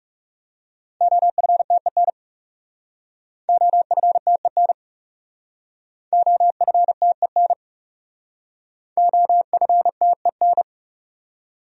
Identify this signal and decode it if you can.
Morse code, heard as a keyed tone, about 38 wpm, 700 Hz. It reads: OFTEN OFTEN OFTEN OFTEN
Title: Speed-Racing Top 300 Words 30wpm